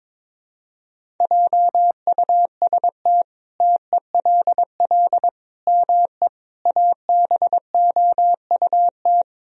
Morse code, heard as a keyed tone, about 22 wpm, 700 Hz. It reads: JUST TELL ME ABOUT
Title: Sentences from Top 1000 Words - Review Part 1 - 22wpm